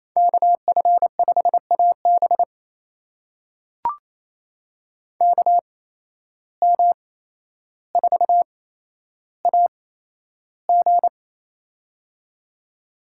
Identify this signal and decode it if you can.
Morse code, heard as a keyed tone, about 28 wpm, spaced as for 12 wpm, 700 Hz. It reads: KF5AB KM4AG
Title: USA Call Signs - 5 Characters - 8x Character Spacing - 28wpm